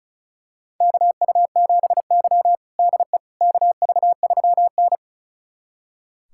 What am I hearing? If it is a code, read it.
Morse code, heard as a keyed tone, about 34 wpm, 700 Hz. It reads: KU7Y DE KV3N
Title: USA Callsign DE Callsign - 4 Characters - 35wpm